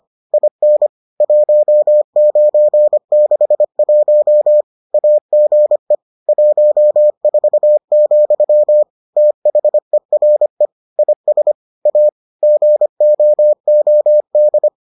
Morse code, heard as a keyed tone, about 25 wpm, 600 Hz. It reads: IN 1961 AGE 14, THERE IS A GOOD